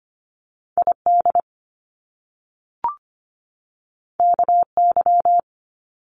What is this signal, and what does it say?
Morse code, heard as a keyed tone, about 25 wpm, 700 Hz. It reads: ID KY
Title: US States 25wpm